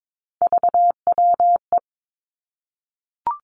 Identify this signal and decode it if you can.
Morse code, heard as a keyed tone, about 22 wpm, 700 Hz. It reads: VWE
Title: Three Challenging Letter-Number Combinations 22wpm